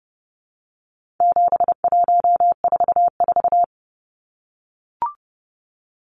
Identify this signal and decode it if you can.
Morse code, heard as a keyed tone, about 30 wpm, 700 Hz. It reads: 7144
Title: Numbers - 4-Digit 30wpm